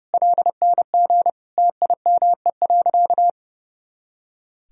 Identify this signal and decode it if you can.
Morse code, heard as a keyed tone, about 30 wpm, 700 Hz. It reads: LNG TIME.
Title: Sentences from Top 400 Words and Abbreviations 30wpm